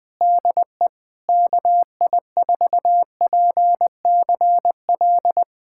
Morse code, heard as a keyed tone, about 21 wpm, 700 Hz. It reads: DE KI4PCL
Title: USA Callsign DE Callsign 20wpm